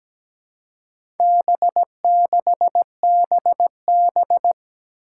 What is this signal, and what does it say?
Morse code, heard as a keyed tone, about 17 wpm, 700 Hz. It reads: B6BB
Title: Four Challenging Letter-Number Combinations 17wpm